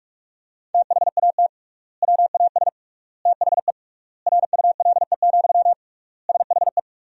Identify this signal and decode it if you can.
Morse code, heard as a keyed tone, about 46 wpm, 700 Hz. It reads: THAT WAS THE RULE, SHE
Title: 3x Word-Spacing - Sentences from Top 400 Words - 45wpm